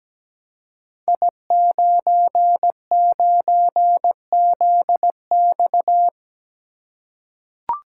Morse code, heard as a keyed tone, about 17 wpm, 700 Hz. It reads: I99ZX
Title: Code Groups - 5 Letter:Numbers - 17wpm